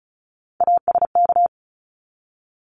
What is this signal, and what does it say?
Morse code, heard as a keyed tone, about 34 wpm, 700 Hz. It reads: ASK
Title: ED Suffix - Top 100 Words with Base Words 35wpm